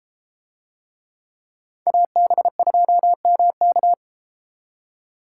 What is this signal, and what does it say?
Morse code, heard as a keyed tone, about 33 wpm, 700 Hz. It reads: AB2MK